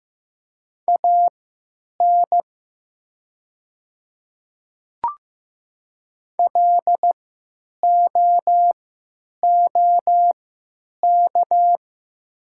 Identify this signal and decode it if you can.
Morse code, heard as a keyed tone, about 15 wpm, spaced as for 9 wpm, 700 Hz. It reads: AN LOOK
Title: 3x Character Spacing - Top 100 Words - 15wpm